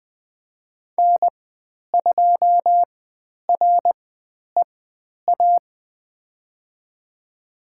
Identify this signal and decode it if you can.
Morse code, heard as a keyed tone, about 20 wpm, spaced as for 10 wpm, 700 Hz. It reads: N2REA
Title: 102 Lesson 17g - Letter u - Call Signs 20wpm